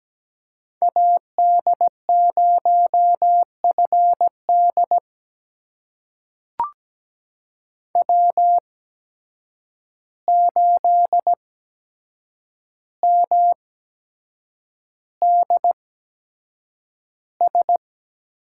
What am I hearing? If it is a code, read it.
Morse code, heard as a keyed tone, about 17 wpm, spaced as for 7 wpm, 700 Hz. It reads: AD0FD W8MDS
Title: USA Call Signs - 5 Characters - 8x Character Spacing - 17wpm